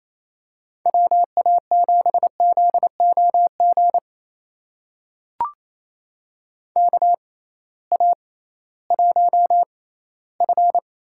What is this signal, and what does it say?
Morse code, heard as a keyed tone, about 28 wpm, spaced as for 16 wpm, 700 Hz. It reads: WA7ZOG KA1F
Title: USA Call Signs - 6 Characters - 6x Character Spacing - 28wpm